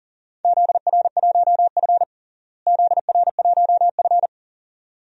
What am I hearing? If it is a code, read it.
Morse code, heard as a keyed tone, about 40 wpm, 700 Hz. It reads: ZR1F ZR1F
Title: DXCC-100 Call Signs - 4 Characters - Repeated 2x - 40wpm